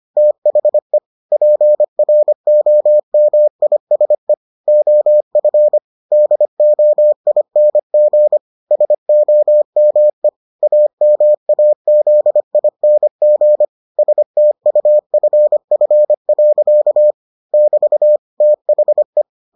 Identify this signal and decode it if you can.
Morse code, heard as a keyed tone, about 25 wpm, 600 Hz. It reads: THE PROMISE OF DOING SOME AMAZING STUFF. <BT> THE